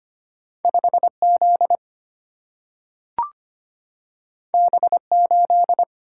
Morse code, heard as a keyed tone, about 25 wpm, 700 Hz. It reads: 5Z B8
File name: Two Letter-Number Combinations - 25wpm